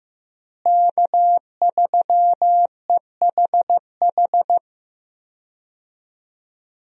Morse code, heard as a keyed tone, about 15 wpm, 700 Hz. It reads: K3EHH
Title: USA Call Signs - 5 Characters - Repeated 3x - 15wpm